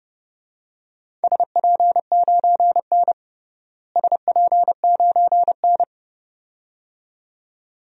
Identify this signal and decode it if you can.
Morse code, heard as a keyed tone, about 30 wpm, 700 Hz. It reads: SP9N SP9N
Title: DXCC-100 Call Signs - 4 Characters - Repeated 2x - 30wpm